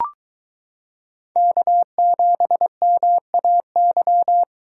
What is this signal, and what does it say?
Morse code, heard as a keyed tone, about 23 wpm, 700 Hz. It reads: K7MAY